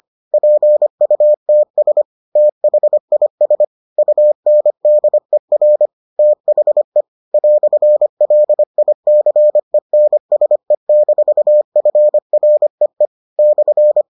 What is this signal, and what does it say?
Morse code, heard as a keyed tone, about 25 wpm, 600 Hz. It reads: PUTS THIS UNDER THE "LICENSE-FREE /